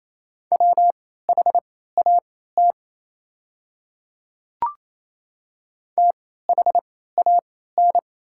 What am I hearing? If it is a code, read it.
Morse code, heard as a keyed tone, about 28 wpm, spaced as for 16 wpm, 700 Hz. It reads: WHAT THAN
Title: 3x Character Spacing - Top 100 Words - 28wpm